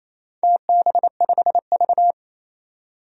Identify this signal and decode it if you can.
Morse code, heard as a keyed tone, about 28 wpm, 700 Hz. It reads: TB5V
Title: DXCC-100 Call Signs - 4 Characters - 28wpm